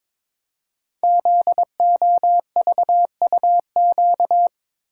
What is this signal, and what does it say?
Morse code, heard as a keyed tone, about 22 wpm, 700 Hz. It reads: ZOVUQ
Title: Code Groups - 5 Letters - 22wpm